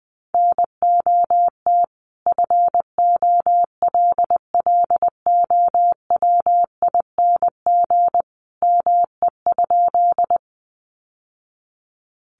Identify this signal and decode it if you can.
Morse code, heard as a keyed tone, about 19 wpm, 700 Hz. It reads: NOT FOLLOWING ME?
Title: Sentences from Top 700 Words 20wpm